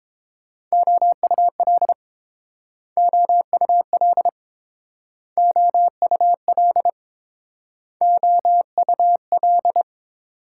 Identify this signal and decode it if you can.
Morse code, heard as a keyed tone, about 30 wpm, 700 Hz. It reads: OUL OUL OUL OUL